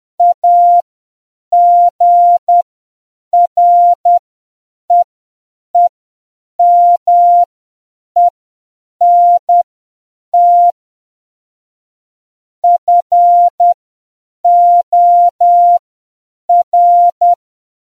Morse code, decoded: AGREEMENT FOR